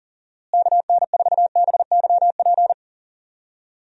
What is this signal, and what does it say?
Morse code, heard as a keyed tone, about 40 wpm, 700 Hz. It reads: KN4BYP
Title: USA Call Signs - 6 Characters - Repeated 2x - 40wpm